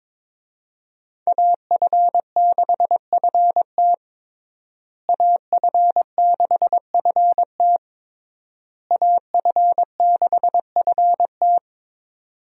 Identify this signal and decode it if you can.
Morse code, heard as a keyed tone, about 22 wpm, 700 Hz. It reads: AF6FT AF6FT AF6FT
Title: USA Call Signs - 5 Characters - Repeated 3x - 22wpm